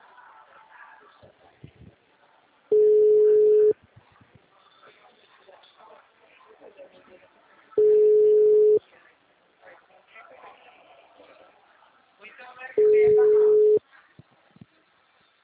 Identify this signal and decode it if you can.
Ringback tone (Europe, 425 Hz), with speech.